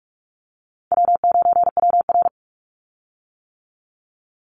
Morse code, heard as a keyed tone, about 46 wpm, 700 Hz. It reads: W9WR
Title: Easy Call Signs - 45wpm